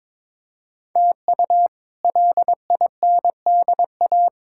Morse code, heard as a keyed tone, about 22 wpm, 700 Hz. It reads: TU LINDA
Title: K1USN SST QSO Practice - 22wpm